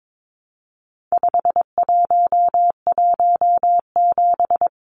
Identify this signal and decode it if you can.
Morse code, heard as a keyed tone, about 22 wpm, 700 Hz. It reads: 5117
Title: Numbers - 4-Digit 22wpm